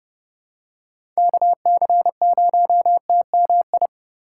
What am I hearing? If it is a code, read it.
Morse code, heard as a keyed tone, about 32 wpm, 700 Hz. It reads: KC0TMI
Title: USA Call Signs - 6 Characters - 2x Character Spacing - 30wpm